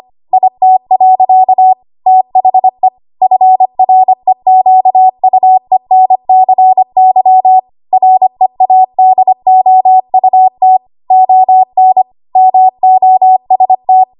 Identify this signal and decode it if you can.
Morse code, heard as a keyed tone, about 25 wpm, 750 Hz. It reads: IT. THE FREQUENCY READOUT ON MOST
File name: Archive / 25 WPM / 25WPM